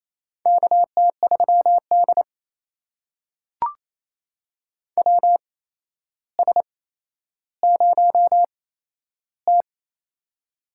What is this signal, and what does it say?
Morse code, heard as a keyed tone, about 28 wpm, spaced as for 12 wpm, 700 Hz. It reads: KT3D WS0T
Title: USA Call Signs - 4 Characters - 8x Character Spacing - 28wpm